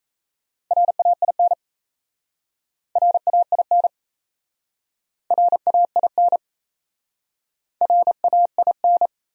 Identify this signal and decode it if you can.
Morse code, heard as a keyed tone, about 37 wpm, 700 Hz. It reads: RAIN RAIN RAIN RAIN